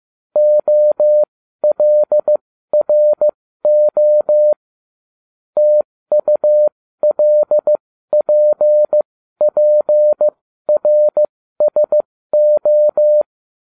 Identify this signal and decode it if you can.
Morse code, heard as a keyed tone, about 15 wpm, spaced as for 13 wpm, 600 Hz. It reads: OLRO TULPPRSO